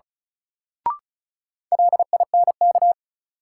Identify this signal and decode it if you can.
Morse code, heard as a keyed tone, about 36 wpm, 700 Hz. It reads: LINK
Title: Sentences from Top 1500 Words - New Word Drill Alt - 35wpm